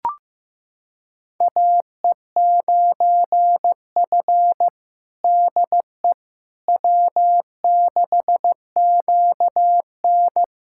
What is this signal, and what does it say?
Morse code, heard as a keyed tone, about 15 wpm, 700 Hz. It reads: AE9F DE W6QN
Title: USA Callsign DE Callsign - 4 Characters - 15wpm